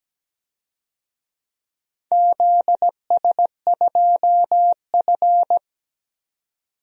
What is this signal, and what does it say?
Morse code, heard as a keyed tone, about 17 wpm, 700 Hz. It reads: ZS2F
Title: DXCC-100 Call Signs - 4 Characters - 17wpm